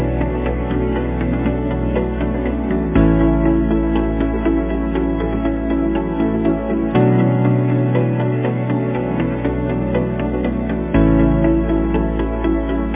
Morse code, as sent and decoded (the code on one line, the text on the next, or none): none